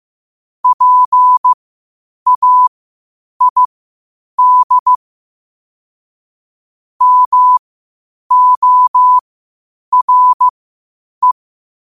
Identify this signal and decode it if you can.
Morse code, heard as a keyed tone, about 15 wpm, spaced as for 8 wpm, 1000 Hz. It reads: PAID MORE
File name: qotdfiles / 8wpm